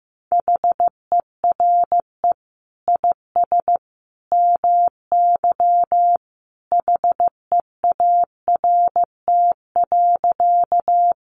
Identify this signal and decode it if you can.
Morse code, heard as a keyed tone, about 15 wpm, 700 Hz. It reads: HERE IS MY HEART.
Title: Sentences from Top 1200 Words 15wpm